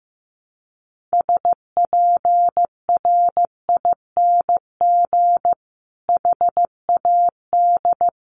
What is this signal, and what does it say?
Morse code, heard as a keyed tone, about 15 wpm, 700 Hz. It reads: SPRING HAD
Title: Sentences from Top 1200 Words 15wpm